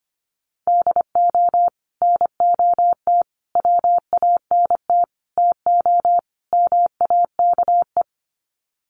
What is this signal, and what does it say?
Morse code, heard as a keyed tone, about 25 wpm, 690 Hz. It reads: DO NOT WANT TO MAKE
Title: Sets of 5 Words Encore 25wpm